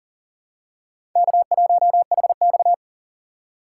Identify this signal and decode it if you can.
Morse code, heard as a keyed tone, about 40 wpm, 700 Hz. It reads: K1HX
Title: USA Call Signs - 4 Characters - 3x Character Spacing - 40wpm